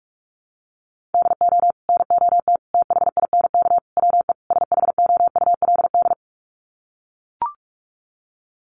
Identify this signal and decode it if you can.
Morse code, heard as a keyed tone, about 44 wpm, 700 Hz. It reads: DO NOT THINK WE SHOULD
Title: Sets of 5 Words Encore 45wpm